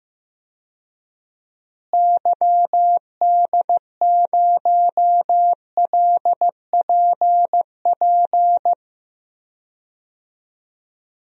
Morse code, heard as a keyed tone, about 15 wpm, 700 Hz. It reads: YD0LPP
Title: DXCC-100 Call Signs - 6 Characters - Repeated 2x - 15wpm